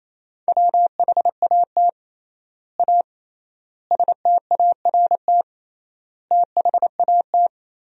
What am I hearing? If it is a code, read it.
Morse code, heard as a keyed tone, about 27 wpm, 700 Hz. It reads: WHAT A START THAT
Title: 3x Word-Spacing - Sentences from Top 300 Words - 28wpm